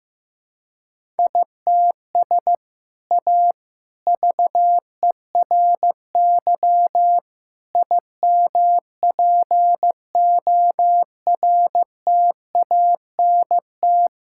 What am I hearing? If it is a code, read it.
Morse code, heard as a keyed tone, about 15 wpm, 700 Hz. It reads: ITS A VERY IMPORTANT
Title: Sentences from Top 1000 Words - Review Part 5 - 15wpm